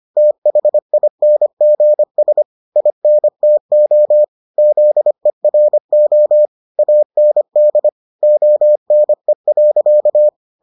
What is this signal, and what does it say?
Morse code, heard as a keyed tone, about 25 wpm, 600 Hz. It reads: THINGS INTO ZERO AND ONE.